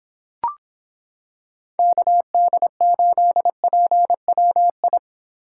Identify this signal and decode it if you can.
Morse code, heard as a keyed tone, about 26 wpm, 700 Hz. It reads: KD8PWI